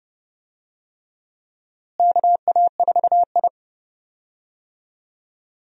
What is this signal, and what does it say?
Morse code, heard as a keyed tone, about 32 wpm, 700 Hz. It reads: KA4I
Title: USA Call Signs - 4 Characters - Repeated 3x - 30wpm